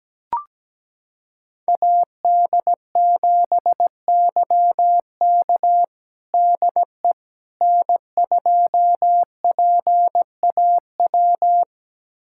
Morse code, heard as a keyed tone, about 17 wpm, 700 Hz. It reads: AD7YK DE N2PAW
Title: USA Callsign DE Callsign - 5 Characters - 17wpm